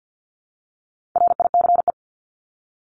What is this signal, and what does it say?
Morse code, heard as a keyed tone, about 50 wpm, 700 Hz. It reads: RICE
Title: Top 1501-1750 Words 50wpm